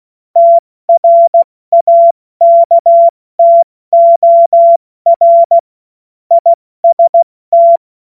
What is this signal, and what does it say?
Morse code, heard as a keyed tone, about 16 wpm, 680 Hz. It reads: TRAKTOR IST
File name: Witz-08-Juli-2025-70-F80